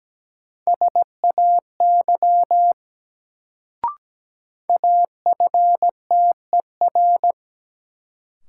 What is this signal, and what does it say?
Morse code, heard as a keyed tone, about 17 wpm, 700 Hz. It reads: SAY AFTER
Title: Sentences from Top 200 Words - New Word Drill - 17wpm